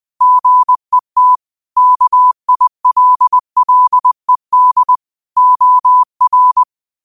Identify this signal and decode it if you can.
Morse code, heard as a keyed tone, about 20 wpm, 1000 Hz. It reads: GET KILLED OR
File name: qotdfiles / 20wpm